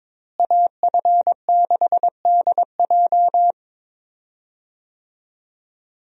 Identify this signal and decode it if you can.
Morse code, heard as a keyed tone, about 21 wpm, 700 Hz. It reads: AF6DJ
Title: USA Call Signs - 5 Characters - Repeated 3x - 22wpm